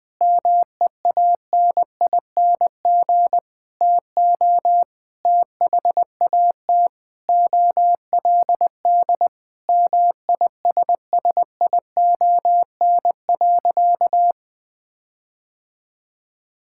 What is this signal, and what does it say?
Morse code, heard as a keyed tone, about 20 wpm, 700 Hz. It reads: MEANING TO THAT OLD MISSION.
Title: Sentences from Top 1600 Words 20wpm